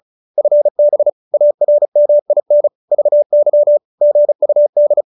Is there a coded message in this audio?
Yes: Morse code from 0.4 to 5.0 s, heard as a keyed tone, about 35 wpm, 590 Hz. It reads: FB ARMIN VY GUD